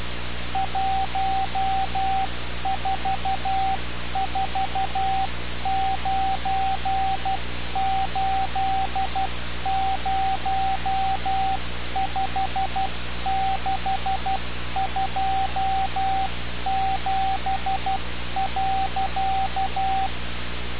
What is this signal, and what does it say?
Morse code, heard as a keyed tone, about 12 wpm, 770 Hz, with radio noise. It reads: 1449805627.
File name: radiocap / REC z